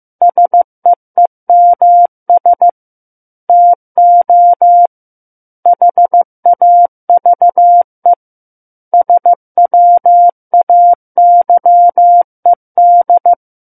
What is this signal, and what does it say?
Morse code, heard as a keyed tone, about 15 wpm, 700 Hz. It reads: SEEMS TO HAVE SWAYED
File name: Book One / 06-Chapter 6 - 15 WPM